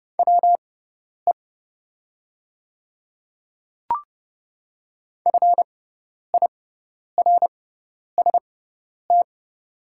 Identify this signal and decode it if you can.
Morse code, heard as a keyed tone, about 30 wpm, spaced as for 10 wpm, 700 Hz. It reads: WE FIRST
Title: 6x Character Spacing - Top 100 Words - 30wpm